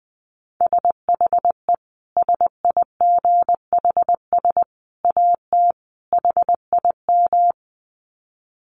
Morse code, heard as a keyed tone, about 20 wpm, 700 Hz. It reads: SHE SIGHS AT HIM
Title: Dit Challenge 20wpm